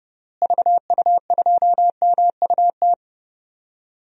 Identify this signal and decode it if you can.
Morse code, heard as a keyed tone, about 32 wpm, 700 Hz. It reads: VU2MUT
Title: DXCC-100 Call Signs - 6 Characters - Repeated 2x - 30wpm